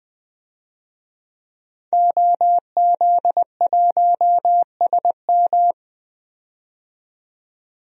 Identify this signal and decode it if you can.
Morse code, heard as a keyed tone, about 21 wpm, 700 Hz. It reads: OZ1SM
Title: DXCC-100 Call Signs - 5 Characters - 20wpm